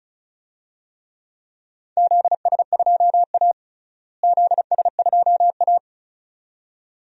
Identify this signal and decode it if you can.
Morse code, heard as a keyed tone, about 36 wpm, 700 Hz. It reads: ZS2A ZS2A
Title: DXCC-100 Call Signs - 4 Characters - Repeated 2x - 35wpm